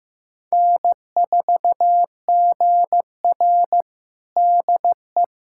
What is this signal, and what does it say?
Morse code, heard as a keyed tone, about 15 wpm, 700 Hz. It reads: N4GR DE
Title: USA Callsign DE Callsign - 4 Characters - 15wpm